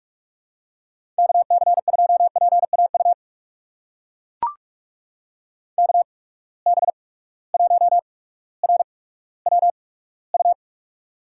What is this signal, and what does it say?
Morse code, heard as a keyed tone, about 44 wpm, spaced as for 19 wpm, 700 Hz. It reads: KC2PAU KB1RWU
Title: USA Call Signs - 6 Characters - 8x Character Spacing - 45wpm